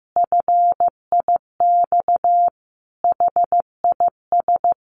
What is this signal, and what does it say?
Morse code, heard as a keyed tone, about 15 wpm, 700 Hz. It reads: FIX HIS
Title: Sentences from Top 1400 Words 15wpm